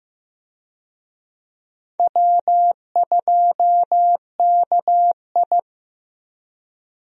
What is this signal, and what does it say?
Morse code, heard as a keyed tone, about 15 wpm, 700 Hz. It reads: W2KI
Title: USA Call Signs - 4 Characters - Repeated 2x - 15wpm